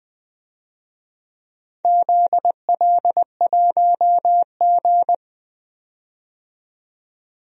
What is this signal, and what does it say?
Morse code, heard as a keyed tone, about 20 wpm, 700 Hz. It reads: ZL1G
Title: DXCC-100 Call Signs - 4 Characters - 20wpm